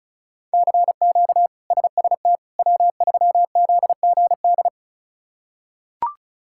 Morse code, heard as a keyed tone, about 34 wpm, 700 Hz. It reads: CQ SST W3ZGD
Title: K1USN SST QSO Practice - 35wpm